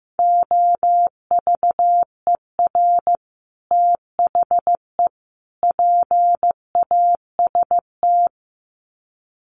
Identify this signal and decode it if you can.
Morse code, heard as a keyed tone, about 15 wpm, 700 Hz. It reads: OVER THE PAST
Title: Sets of 3 Words 15wpm